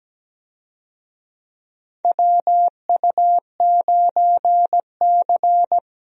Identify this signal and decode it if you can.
Morse code, heard as a keyed tone, about 17 wpm, 700 Hz. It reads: WU9C
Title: USA Call Signs - 4 Characters - Repeated 2x - 17wpm